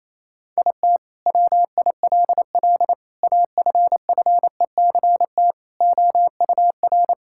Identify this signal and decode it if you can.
Morse code, heard as a keyed tone, about 28 wpm, 700 Hz. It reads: IT WILL AFFECT OUR
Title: Sentences from Top 1000 Words - Review Part 6 - 28wpm